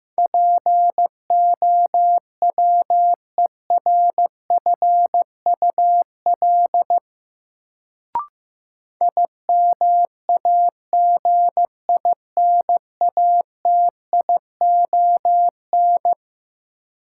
Morse code, heard as a keyed tone, about 15 wpm, 700 Hz. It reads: POWERFUL IMAGINATION
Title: Sentences from Top 1300 Words - New Word Drill Alt - 15wpm